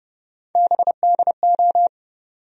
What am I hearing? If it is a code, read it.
Morse code, heard as a keyed tone, about 30 wpm, 700 Hz. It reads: BDO
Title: USA Call Sign Postfixes - 3 Letters 30wpm